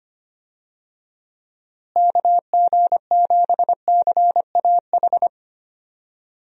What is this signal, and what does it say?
Morse code, heard as a keyed tone, about 25 wpm, 700 Hz. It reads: KG7CAH
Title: USA Call Signs - 6 Characters - Repeated 2x - 25wpm